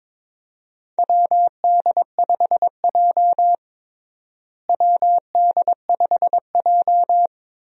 Morse code, heard as a keyed tone, about 21 wpm, 700 Hz. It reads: WD5J WD5J
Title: USA Call Signs - 4 Characters - Repeated 2x - 22wpm